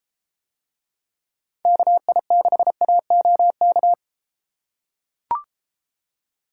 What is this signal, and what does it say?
Morse code, heard as a keyed tone, about 31 wpm, 700 Hz. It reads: KI6AOK